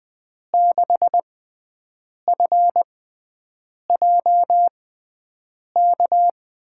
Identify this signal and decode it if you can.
Morse code, heard as a keyed tone, about 20 wpm, spaced as for 7 wpm, 700 Hz. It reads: 6FJK
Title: USA Call Signs - 6x Character Spacing - 20wpm